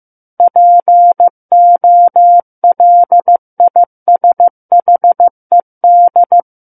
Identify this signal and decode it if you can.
Morse code, heard as a keyed tone, about 15 wpm, 700 Hz. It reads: POLISHED
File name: Book One / 06-Chapter 6 - 15 WPM